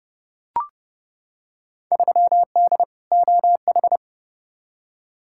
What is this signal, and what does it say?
Morse code, heard as a keyed tone, about 30 wpm, 700 Hz. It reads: 3D OH
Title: Field Day Exchanges - 30wpm